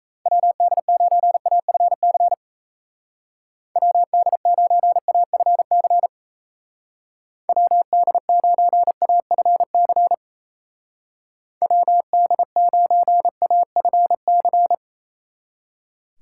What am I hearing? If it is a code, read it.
Morse code, heard as a keyed tone, about 35 wpm, 700 Hz. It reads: WD9AFC WD9AFC WD9AFC WD9AFC